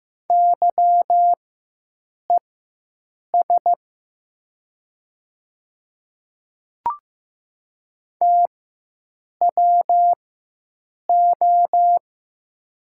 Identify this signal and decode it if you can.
Morse code, heard as a keyed tone, about 15 wpm, spaced as for 7 wpm, 700 Hz. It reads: YES TWO
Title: 4x Character Spacing - Top 500 Words - 15wpm